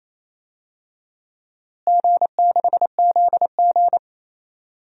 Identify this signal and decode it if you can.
Morse code, heard as a keyed tone, about 27 wpm, 700 Hz. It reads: G6ZG
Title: DXCC-100 Call Signs - 4 Characters - Repeated 3x - 28wpm